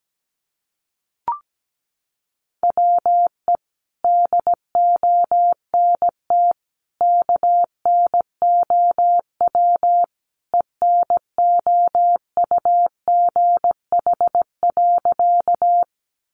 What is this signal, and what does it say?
Morse code, heard as a keyed tone, about 17 wpm, 700 Hz. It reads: WE DONT KNOW ENOUGH.
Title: Sentences from Top 400 Words 17wpm